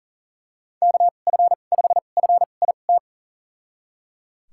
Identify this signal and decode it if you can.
Morse code, heard as a keyed tone, about 37 wpm, spaced as for 28 wpm, 700 Hz. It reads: KF5FIT
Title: USA Call Signs - 6 Characters - 2x Character Spacing - 40wpm